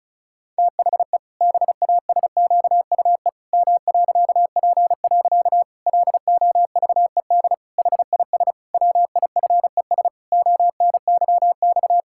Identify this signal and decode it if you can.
Morse code, heard as a keyed tone, about 34 wpm, 700 Hz. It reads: THE BASQUE M.P. LOVED HIS WIFES ONYX
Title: Pangram Challenge v2 - 35wpm